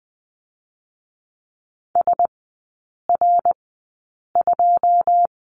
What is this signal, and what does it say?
Morse code, heard as a keyed tone, about 20 wpm, spaced as for 8 wpm, 690 Hz. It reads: SR2
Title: 087 Lesson 15e - Number 5 - Three Characters 20wpm